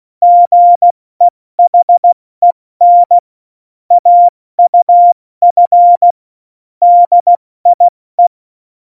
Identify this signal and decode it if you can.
Morse code, heard as a keyed tone, about 16 wpm, spaced as for 14 wpm, 700 Hz. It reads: GEHEN AUF DIE
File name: Morsekurs / Witze / Witz 18 August 2020, 70 F80